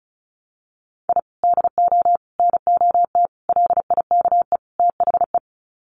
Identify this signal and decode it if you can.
Morse code, heard as a keyed tone, about 34 wpm, 700 Hz. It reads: I DO NOT LIKE THE
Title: Sets of 5 Words Encore 35wpm